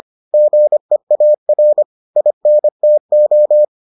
Morse code, heard as a keyed tone, about 25 wpm, 600 Hz. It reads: GEAR INTO